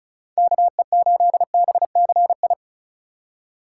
Morse code, heard as a keyed tone, about 34 wpm, 700 Hz. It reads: KE8BCI
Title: USA Call Signs - 6 Characters - 2x Character Spacing - 35wpm